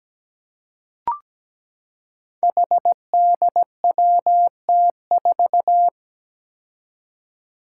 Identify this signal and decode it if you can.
Morse code, heard as a keyed tone, about 17 wpm, 700 Hz. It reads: HDWT4
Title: Five Challenging Letter-Number Combinations 17wpm